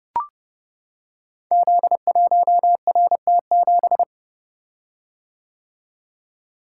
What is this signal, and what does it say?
Morse code, heard as a keyed tone, about 30 wpm, 700 Hz. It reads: Z1RT7
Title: Code Groups - 5 Letter:Numbers - 30wpm